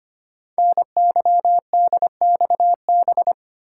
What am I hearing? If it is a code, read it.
Morse code, heard as a keyed tone, about 25 wpm, 700 Hz. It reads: NYDXB
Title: Code Groups - 5 Letters - 25wpm